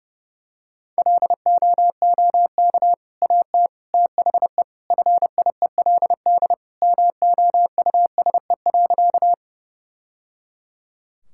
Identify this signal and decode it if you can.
Morse code, heard as a keyed tone, about 30 wpm, 700 Hz. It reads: LOOK AT THE FIELD MOUSE.
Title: Sentences from Top 1000 Words - Review Part 5 - 30wpm